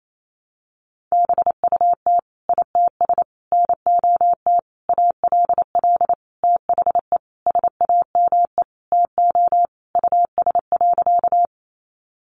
Morse code, heard as a keyed tone, about 28 wpm, 700 Hz. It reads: BUT ITS NOT ALL THE SAME TO US.
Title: Sentences from Top 600 Words 28wpm